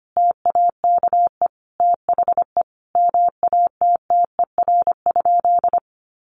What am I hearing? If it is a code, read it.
Morse code, heard as a keyed tone, about 25 wpm, 700 Hz. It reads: TAKE THE MATTER?
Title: Sentences from Top 700 Words 25wpm